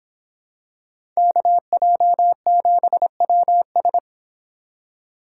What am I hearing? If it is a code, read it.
Morse code, heard as a keyed tone, about 25 wpm, 700 Hz. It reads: KJ7WS